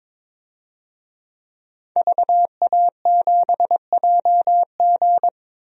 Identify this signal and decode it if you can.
Morse code, heard as a keyed tone, about 22 wpm, 700 Hz. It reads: VA7JG
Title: DXCC-100 Call Signs - 5 Characters - 22wpm